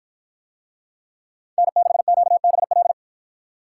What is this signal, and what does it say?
Morse code, heard as a keyed tone, about 53 wpm, 700 Hz. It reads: N6QBL